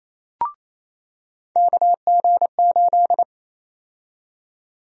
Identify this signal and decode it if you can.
Morse code, heard as a keyed tone, about 28 wpm, 700 Hz. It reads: KG8
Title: USA Call Sign Prefixes - 28wpm